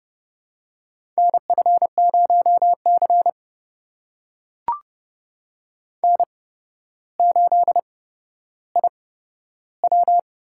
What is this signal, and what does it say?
Morse code, heard as a keyed tone, about 30 wpm, spaced as for 13 wpm, 700 Hz. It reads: NF0C N8IW
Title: USA Call Signs - 4 Characters - 8x Character Spacing - 30wpm